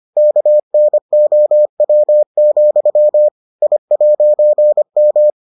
Morse code, heard as a keyed tone, about 25 wpm, 600 Hz. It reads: KNOW, I'M